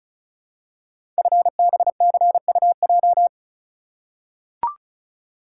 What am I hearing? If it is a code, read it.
Morse code, heard as a keyed tone, about 36 wpm, 700 Hz. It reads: FBCUJ
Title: Code Groups - 5 Letters - 35wpm